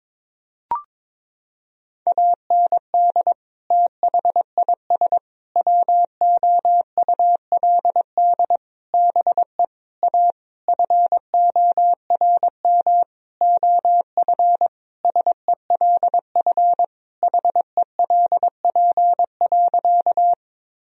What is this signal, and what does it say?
Morse code, heard as a keyed tone, about 22 wpm, 700 Hz. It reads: AND THIS WOULD BE A FORM OF SELF HELP.
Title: Sentences from Top 1000 Words - Review Part 1 - 22wpm